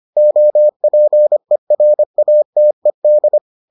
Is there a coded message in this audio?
Yes: Morse code from 0.2 to 3.4 s, heard as a keyed tone, about 25 wpm, 600 Hz. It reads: OPERATED